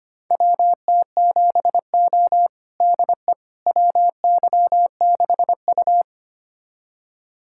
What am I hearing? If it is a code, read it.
Morse code, heard as a keyed tone, about 25 wpm, 700 Hz. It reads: WT7O DE WY6U
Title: USA Callsign DE Callsign - 4 Characters - 25wpm